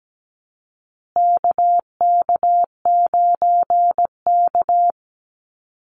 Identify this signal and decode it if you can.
Morse code, heard as a keyed tone, about 17 wpm, 700 Hz. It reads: KK9K
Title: USA Call Signs Repeated 3x 17wpm